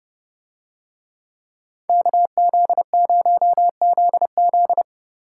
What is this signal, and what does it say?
Morse code, heard as a keyed tone, about 30 wpm, 700 Hz. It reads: KZ0ZZ